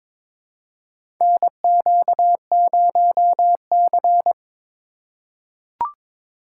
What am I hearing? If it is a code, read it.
Morse code, heard as a keyed tone, about 22 wpm, 700 Hz. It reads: NQ0C
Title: USA Call Signs - 4 Characters - 4x Character Spacing - 22wpm